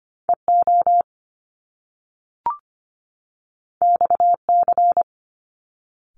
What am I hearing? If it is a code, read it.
Morse code, heard as a keyed tone, about 25 wpm, 700 Hz. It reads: E O X C
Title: Top Two Letter Combinations in Words 25wpm